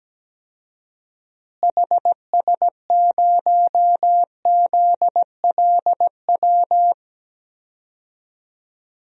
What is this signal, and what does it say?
Morse code, heard as a keyed tone, about 17 wpm, 700 Hz. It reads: HS0ZLW
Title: DXCC-100 Call Signs - 6 Characters - Repeated 2x - 17wpm